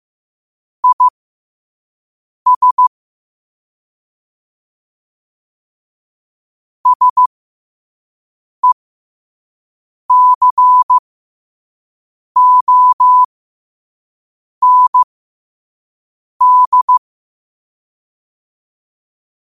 Morse code, heard as a keyed tone, about 15 wpm, spaced as for 5 wpm, 1000 Hz. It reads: IS SECOND
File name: qotdfiles / 5wpm